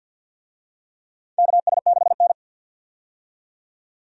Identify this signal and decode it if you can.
Morse code, heard as a keyed tone, about 50 wpm, 700 Hz. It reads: KS6N
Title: USA Call Signs - 4 Characters - 50wpm